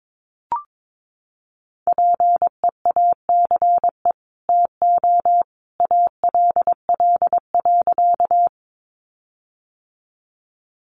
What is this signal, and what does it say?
Morse code, heard as a keyed tone, about 22 wpm, 700 Hz. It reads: PEACE TO ALL.
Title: Sentences from Top 1500 Words 22wpm